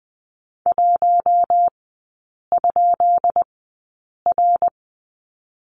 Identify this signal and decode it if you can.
Morse code, heard as a keyed tone, about 20 wpm, spaced as for 8 wpm, 700 Hz. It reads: 1?R
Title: 172 Lesson 28e - Number 9 - Three Characters 20wpm